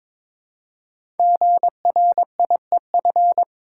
Morse code, heard as a keyed tone, about 22 wpm, 700 Hz. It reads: GRIEF
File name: 5 Letter Words - First 3 Letters Given - 22wpm